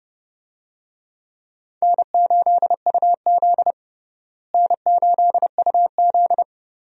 Morse code, heard as a keyed tone, about 30 wpm, 700 Hz. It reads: N8UZ N8UZ
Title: USA Call Signs - 4 Characters - Repeated 3x - 30wpm